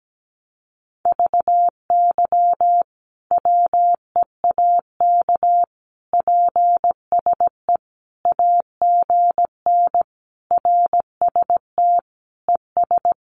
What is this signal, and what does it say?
Morse code, heard as a keyed tone, about 17 wpm, 700 Hz. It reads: VY WEAK PSE AGN RST ES
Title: Common QSO Phrases 17wpm